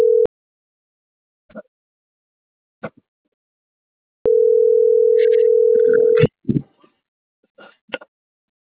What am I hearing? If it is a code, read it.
Ringback tone (North America), 440 + 480 Hz.